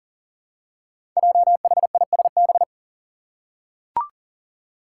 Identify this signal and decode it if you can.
Morse code, heard as a keyed tone, about 39 wpm, 700 Hz. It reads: JHISB
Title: Five Challenging Letter-Number Combinations 40wpm